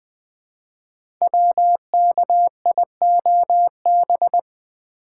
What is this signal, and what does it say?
Morse code, heard as a keyed tone, about 20 wpm, 700 Hz. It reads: WKIOB
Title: Code Groups - 5 Letters - 20wpm